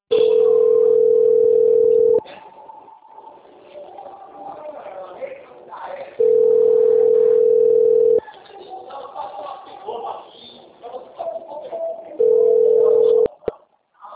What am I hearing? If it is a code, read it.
Ringback tone (North America), 440 + 480 Hz.